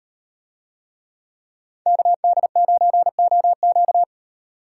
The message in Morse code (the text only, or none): KD9OQ